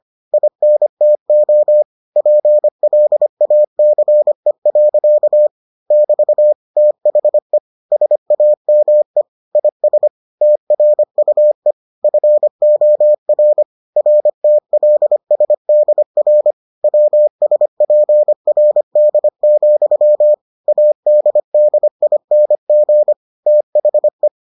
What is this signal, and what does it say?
Morse code, heard as a keyed tone, about 25 wpm, 600 Hz. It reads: INTO PLACE. <BT> THE SAME IS TRUE FOR RTLSDR WSPRD, ADDING THE